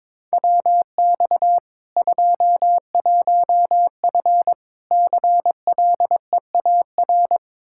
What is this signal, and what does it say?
Morse code, heard as a keyed tone, about 22 wpm, 700 Hz. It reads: WX 21F CLEAR
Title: Practice QSO - Exchange 2 - Info - 22wpm